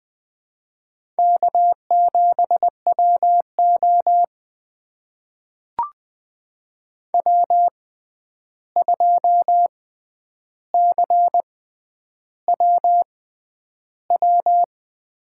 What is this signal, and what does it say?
Morse code, heard as a keyed tone, about 20 wpm, spaced as for 10 wpm, 700 Hz. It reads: K7WO W2CWW
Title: USA Call Signs - 6x Character Spacing - 20wpm